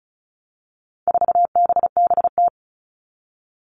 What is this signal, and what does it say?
Morse code, heard as a keyed tone, about 34 wpm, 700 Hz. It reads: 4BBT
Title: All Cut Numbers - 4-Digit 35wpm